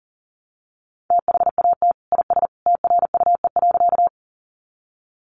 Morse code, heard as a keyed tone, about 40 wpm, 700 Hz. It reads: THAT IS TRUE.
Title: Sentences from Top 400 Words 40wpm